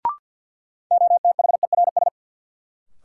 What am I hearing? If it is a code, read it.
Morse code, heard as a keyed tone, about 52 wpm, 700 Hz. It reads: OTHERS
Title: Sentences from Top 1000 Words - Review Part 2 - New Word Drill - 50wpm